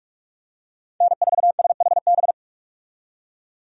Morse code, heard as a keyed tone, about 41 wpm, 700 Hz. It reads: N4SHB
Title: USA Call Signs - 5 Characters - 45wpm